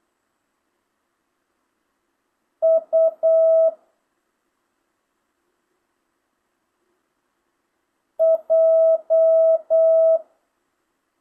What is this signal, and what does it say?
Morse code, heard as a keyed tone, about 8 wpm, 650 Hz. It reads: U J
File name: jh8cbh / 2025cw